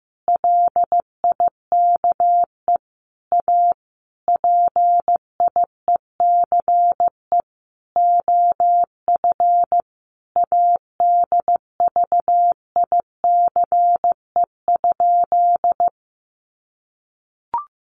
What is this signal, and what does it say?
Morse code, heard as a keyed tone, about 15 wpm, 700 Hz. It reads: LIKE A PIECE OF ADVICE?
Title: Sentences from Top 1100 Words 15wpm